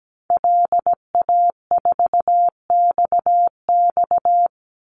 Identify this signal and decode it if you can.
Morse code, heard as a keyed tone, about 17 wpm, 700 Hz. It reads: LA4XX
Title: Call Signs Repeated 3x - 17wpm